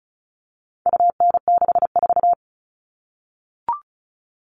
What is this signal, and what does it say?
Morse code, heard as a keyed tone, about 34 wpm, 700 Hz. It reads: UN64
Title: All Cut Numbers - 4-Digit 35wpm